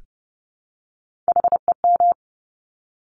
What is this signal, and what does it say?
Morse code, heard as a keyed tone, about 30 wpm, 690 Hz. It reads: HEM
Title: Top Three Letter Combinations in Words 30wpm